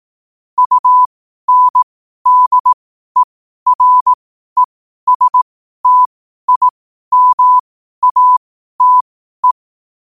Morse code, heard as a keyed tone, about 18 wpm, spaced as for 12 wpm, 1000 Hz. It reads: UNDERESTIMATE
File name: qotdfiles / 12wpm